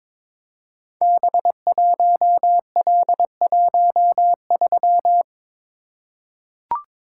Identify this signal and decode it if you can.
Morse code, heard as a keyed tone, about 22 wpm, 700 Hz. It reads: B1L13
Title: Code Groups - 5 Letter:Numbers - 22wpm